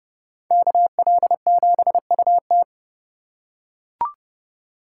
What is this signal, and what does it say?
Morse code, heard as a keyed tone, about 30 wpm, 700 Hz. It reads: KL7UT